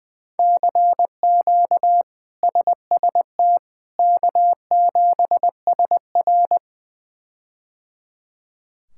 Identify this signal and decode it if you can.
Morse code, heard as a keyed tone, about 20 wpm, 700 Hz. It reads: CQ SST K7SR
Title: K1USN SST QSO Practice - 20wpm